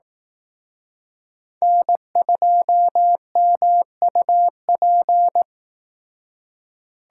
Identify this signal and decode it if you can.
Morse code, heard as a keyed tone, about 18 wpm, 700 Hz. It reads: N2MUP
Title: Speed Racing - USA 1x3 Call Signs 15wpm